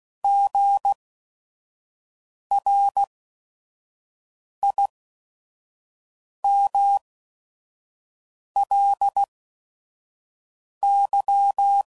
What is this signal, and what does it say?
Morse code, heard as a keyed tone, about 16 wpm, spaced as for 5 wpm, 800 Hz. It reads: GRIMLY
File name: Prefixes / By First Letter / Gs